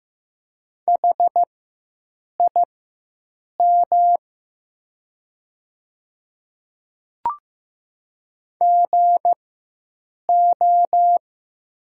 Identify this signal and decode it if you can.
Morse code, heard as a keyed tone, about 16 wpm, spaced as for 7 wpm, 700 Hz. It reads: HIM GO
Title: 4x Character Spacing - Top 100 Words - 15wpm